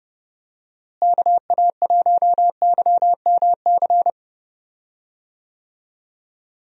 Morse code, heard as a keyed tone, about 30 wpm, 700 Hz. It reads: KA1YMC
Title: USA Call Signs - 6 Characters - Repeated 3x - 30wpm